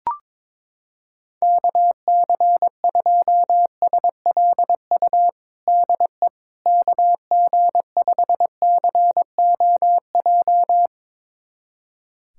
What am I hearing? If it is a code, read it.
Morse code, heard as a keyed tone, about 22 wpm, 700 Hz. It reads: KC2SLU DE KG5COJ
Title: USA Callsign DE Callsign - 6 Characters - 22wpm